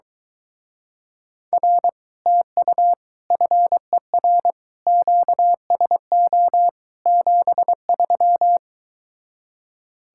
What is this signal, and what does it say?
Morse code, heard as a keyed tone, about 23 wpm, 700 Hz. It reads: R TU FER QSO 73